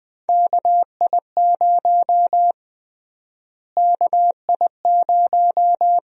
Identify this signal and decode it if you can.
Morse code, heard as a keyed tone, about 20 wpm, 700 Hz. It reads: KI0 KI0
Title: USA Call Sign Prefixes - Repeated 2x - 20wpm